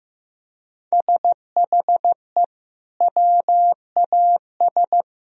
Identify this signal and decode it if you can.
Morse code, heard as a keyed tone, about 16 wpm, 700 Hz. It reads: SHE WAS
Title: Sentences from Top 1000 Words - Review Part 2 - 15wpm